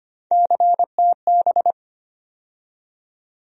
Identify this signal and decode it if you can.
Morse code, heard as a keyed tone, about 25 wpm, 700 Hz. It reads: CTB
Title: USA Call Sign Postfixes - 3 Letters 25wpm